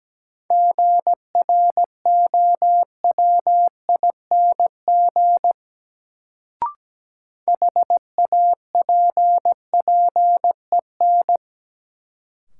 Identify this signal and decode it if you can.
Morse code, heard as a keyed tone, about 17 wpm, 700 Hz. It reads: GROWING HAPPEN
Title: Sets of 5 Words Encore - New Word Drill - 17wpm